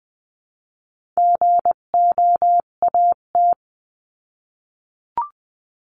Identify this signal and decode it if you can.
Morse code, heard as a keyed tone, about 20 wpm, 700 Hz. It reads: GOAT
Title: 258 Lesson 42c - Words 16-20wpm